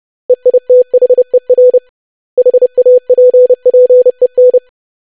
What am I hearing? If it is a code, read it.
Morse code, heard as a keyed tone, about 30 wpm, 500 Hz. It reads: EITHER HAPPEN
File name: lesson / web / basic500-25wpm